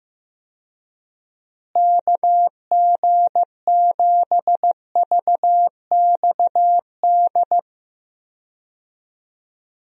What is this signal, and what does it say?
Morse code, heard as a keyed tone, about 15 wpm, 700 Hz. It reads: KG7VXD
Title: USA Call Signs - 6 Characters - Repeated 3x - 15wpm